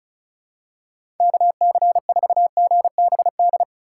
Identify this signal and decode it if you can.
Morse code, heard as a keyed tone, about 34 wpm, 700 Hz. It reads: KC4GBD
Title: USA Call Signs - 2x Character Spacing - 35wpm